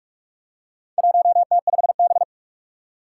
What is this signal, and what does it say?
Morse code, heard as a keyed tone, about 46 wpm, 700 Hz. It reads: 1T5B
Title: Four Challenging Letter-Number Combinations 45wpm